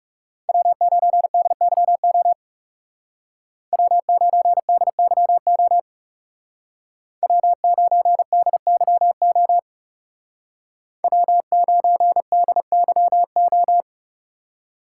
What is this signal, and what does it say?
Morse code, heard as a keyed tone, about 36 wpm, 700 Hz. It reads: W9DYO W9DYO W9DYO W9DYO
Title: Speed Racing - USA 1x3 Call Signs 30wpm